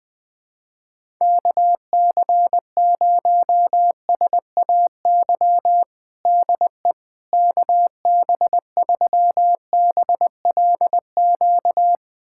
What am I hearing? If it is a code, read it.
Morse code, heard as a keyed tone, about 21 wpm, 700 Hz. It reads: KC0SAY DE KB3BLQ
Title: USA Callsign DE Callsign - 6 Characters - 20wpm